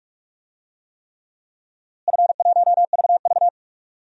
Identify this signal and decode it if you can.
Morse code, heard as a keyed tone, about 46 wpm, 700 Hz. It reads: F1VV
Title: DXCC-100 Call Signs - 4 Characters - Repeated 3x - 45wpm